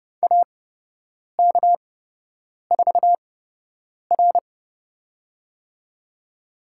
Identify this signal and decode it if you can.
Morse code, heard as a keyed tone, about 30 wpm, spaced as for 8 wpm, 700 Hz. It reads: AK4R
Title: USA Call Signs - 8x Character Spacing - 30wpm